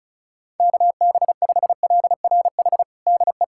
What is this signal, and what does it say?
Morse code, heard as a keyed tone, about 36 wpm, 690 Hz. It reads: KB5LRH DE
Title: USA Callsign DE Callsign 35wpm